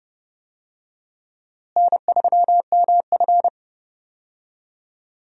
Morse code, heard as a keyed tone, about 32 wpm, 700 Hz. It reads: N3MF
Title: USA Call Signs - 4 Characters - Repeated 3x - 30wpm